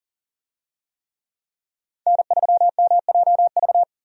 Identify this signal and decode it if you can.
Morse code, heard as a keyed tone, about 40 wpm, 700 Hz. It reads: N3MJV
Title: USA Call Signs - 5 Characters - 40wpm